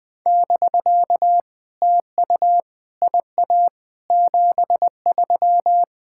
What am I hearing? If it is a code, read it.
Morse code, heard as a keyed tone, about 20 wpm, 700 Hz. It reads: <BK> TU IA 73
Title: POTA QSO Practice - 20wpm